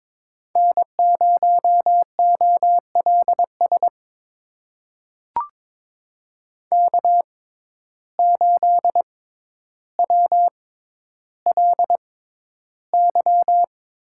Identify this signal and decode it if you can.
Morse code, heard as a keyed tone, about 22 wpm, spaced as for 11 wpm, 700 Hz. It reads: N0OLS K8WLY
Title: USA Call Signs - 5 Characters - 6x Character Spacing - 22wpm